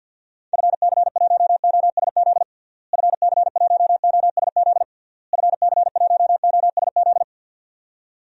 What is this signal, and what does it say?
Morse code, heard as a keyed tone, about 48 wpm, 700 Hz. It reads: FC1OSZ FC1OSZ FC1OSZ
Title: DXCC-100 Call Signs - 6 Characters - Repeated 3x - 50wpm